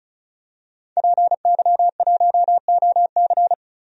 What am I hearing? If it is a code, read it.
Morse code, heard as a keyed tone, about 36 wpm, 700 Hz. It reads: PY1OC
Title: DXCC-100 Call Signs - 5 Characters - 35wpm